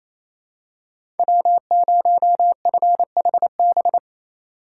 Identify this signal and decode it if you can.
Morse code, heard as a keyed tone, about 28 wpm, 700 Hz. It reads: W0FHB
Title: USA Call Signs - 2x Character Spacing - 28wpm